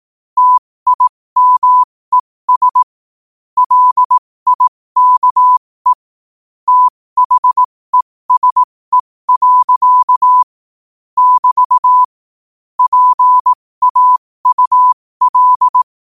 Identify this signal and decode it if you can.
Morse code, heard as a keyed tone, about 18 wpm, spaced as for 15 wpm, 1000 Hz. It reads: TIMES LIKE THESE. <BT> PAUL
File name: qotdfiles / 15wpm